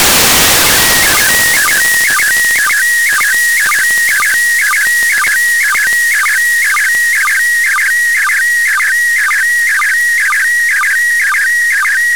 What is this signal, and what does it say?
SSTV (slow-scan television), PD 120.
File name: samples ogg / satnogs